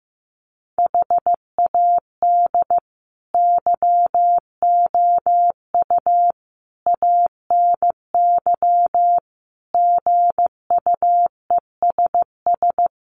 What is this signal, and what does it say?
Morse code, heard as a keyed tone, about 15 wpm, 700 Hz. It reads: HAD YOU ANY GUESS
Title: Sentences from Top 1500 Words 15wpm